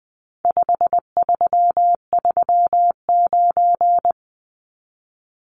Morse code, heard as a keyed tone, about 20 wpm, 700 Hz. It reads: 5339
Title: Numbers - 4-Digit 20wpm